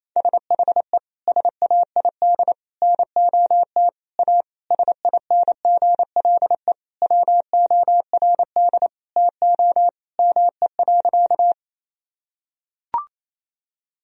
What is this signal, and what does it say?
Morse code, heard as a keyed tone, about 28 wpm, 700 Hz. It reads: SHE SAID NOT A SINGLE WORD TO ME.
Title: Sentences from Top 1000 Words - Review Part 4 - 28wpm